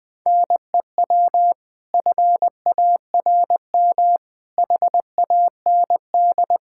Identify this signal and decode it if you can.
Morse code, heard as a keyed tone, about 20 wpm, 700 Hz. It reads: NEW FARM HAND
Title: Pangram Challenge v2 - 20wpm